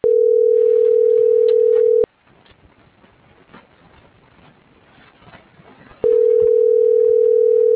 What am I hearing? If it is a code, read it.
Ringback tone (North America), 440 + 480 Hz.